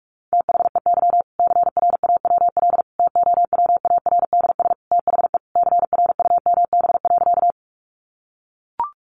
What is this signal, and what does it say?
Morse code, heard as a keyed tone, about 44 wpm, 700 Hz. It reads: THEY CRAWL TOWARDS THE CRUMB.
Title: Additional Easy Phrases 45wpm